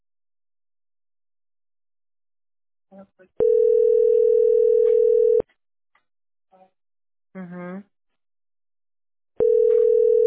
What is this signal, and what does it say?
Ringback tone (North America), 440 + 480 Hz, with speech.
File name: 8eb2244a-3d32-4153-9264